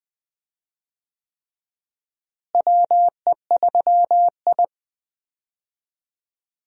Morse code, heard as a keyed tone, about 20 wpm, 700 Hz. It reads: WE3I
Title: USA Call Signs - 4 Characters - Repeated 3x - 20wpm